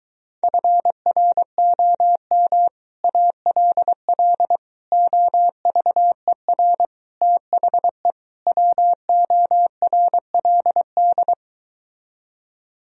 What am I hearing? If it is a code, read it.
Morse code, heard as a keyed tone, about 24 wpm, 700 Hz. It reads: FROM ALL OVER THE WORLD